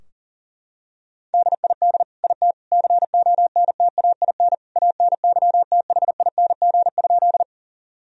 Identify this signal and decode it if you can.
Morse code, heard as a keyed tone, about 40 wpm, 700 Hz. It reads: DID IT CONTAIN ANYTHING?
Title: Sentences from Top 1000 Words - Review Part 2 - 40wpm